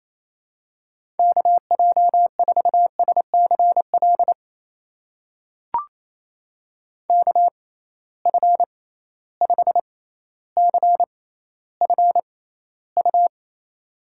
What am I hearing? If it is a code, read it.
Morse code, heard as a keyed tone, about 28 wpm, spaced as for 14 wpm, 700 Hz. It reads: KJ4SCL KF5CFU
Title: USA Call Signs - 6 Characters - 6x Character Spacing - 28wpm